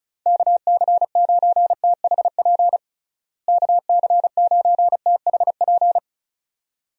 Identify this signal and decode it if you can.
Morse code, heard as a keyed tone, about 34 wpm, 700 Hz. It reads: KC9THP KC9THP
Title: USA Call Signs - 6 Characters - Repeated 2x - 35wpm